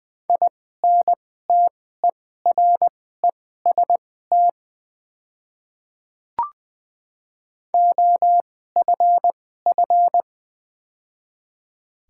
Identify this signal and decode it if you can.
Morse code, heard as a keyed tone, about 20 wpm, spaced as for 15 wpm, 700 Hz. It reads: INTEREST OFF
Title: 2x Character Spacing - Top 500 Words - 20wpm